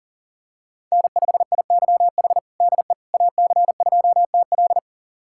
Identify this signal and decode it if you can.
Morse code, heard as a keyed tone, about 37 wpm, 700 Hz. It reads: N5IYH DE AC2TL
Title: USA Callsign DE Callsign - 5 Characters - 40wpm